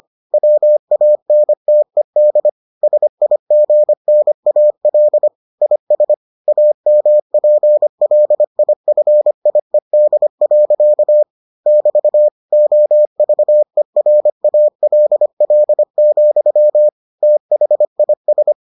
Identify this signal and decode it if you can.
Morse code, heard as a keyed tone, about 25 wpm, 600 Hz. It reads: WANTED SIGNAL IS AMPLIFIED. <BT> OVERALL, THIS